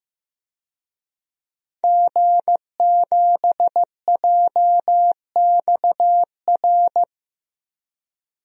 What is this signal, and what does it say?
Morse code, heard as a keyed tone, about 15 wpm, 700 Hz. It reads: G7JXR
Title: DXCC-100 Call Signs - 5 Characters - 15wpm